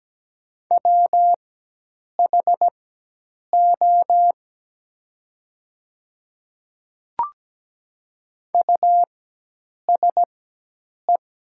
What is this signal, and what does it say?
Morse code, heard as a keyed tone, about 17 wpm, spaced as for 8 wpm, 700 Hz. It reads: WHO USE
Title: 4x Character Spacing - Top 200 Words - 17wpm